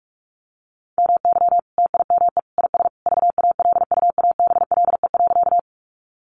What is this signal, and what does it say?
Morse code, heard as a keyed tone, about 44 wpm, 700 Hz. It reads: MY TIME IS VALUABLE.
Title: Sentences from Top 1100 Words 45wpm